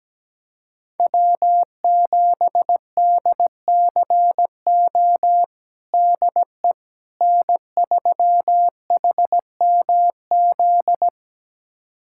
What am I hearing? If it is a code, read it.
Morse code, heard as a keyed tone, about 17 wpm, 700 Hz. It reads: W7DCO DE N3HMZ
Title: USA Callsign DE Callsign - 5 Characters - 17wpm